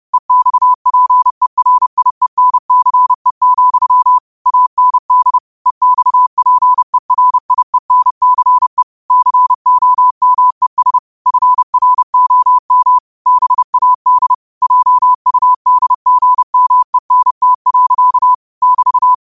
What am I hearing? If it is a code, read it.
Morse code, heard as a keyed tone, about 30 wpm, 1000 Hz. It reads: EXPERIENCE, AND EXPERIENCE COMES FROM BAD JUDGMENT. <BT>